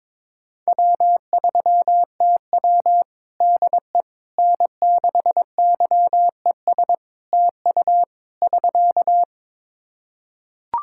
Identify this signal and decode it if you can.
Morse code, heard as a keyed tone, about 22 wpm, 700 Hz. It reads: W3TW DE N6YES TU <SK>
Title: Practice QSO - Exchange 4 - Ending - 22wpm